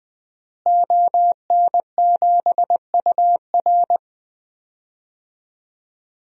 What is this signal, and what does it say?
Morse code, heard as a keyed tone, about 20 wpm, 700 Hz. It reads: ON7UR
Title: DXCC-100 Call Signs - 5 Characters - Repeated 3x - 20wpm